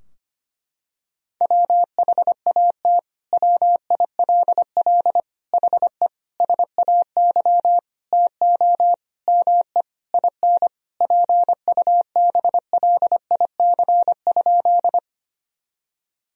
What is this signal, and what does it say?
Morse code, heard as a keyed tone, about 25 wpm, 700 Hz. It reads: WHAT WILL HE SAY TO ME IN PUBLIC?
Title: Sentences from Top 1000 Words - Review Part 3 - 25wpm